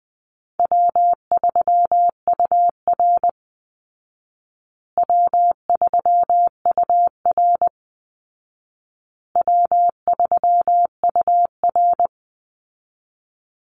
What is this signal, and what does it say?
Morse code, heard as a keyed tone, about 20 wpm, 700 Hz. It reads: W3UR W3UR W3UR
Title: Easy Call Signs Repeated 3x - 20wpm